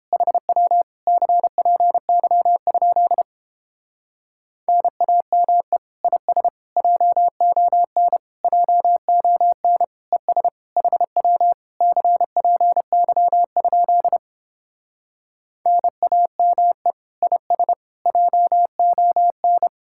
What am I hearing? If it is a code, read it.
Morse code, heard as a keyed tone, about 31 wpm, 700 Hz. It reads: HW CPY? NAME IS JON JON ES HW CPY? NAME IS JON